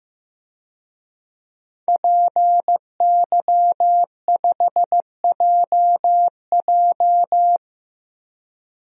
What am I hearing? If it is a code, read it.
Morse code, heard as a keyed tone, about 15 wpm, 700 Hz. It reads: PY5JJ
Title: DXCC-100 Call Signs - 5 Characters - Repeated 2x - 15wpm